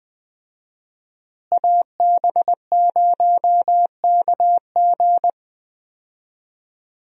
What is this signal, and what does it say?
Morse code, heard as a keyed tone, about 20 wpm, 700 Hz. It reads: AB0KG
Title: USA Call Signs - 5 Characters - Repeated 3x - 20wpm